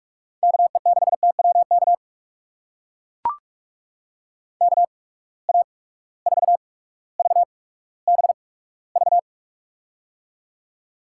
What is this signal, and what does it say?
Morse code, heard as a keyed tone, about 42 wpm, spaced as for 19 wpm, 700 Hz. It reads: KE6TWK KA4VBV
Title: USA Call Signs - 8x Character Spacing - 45wpm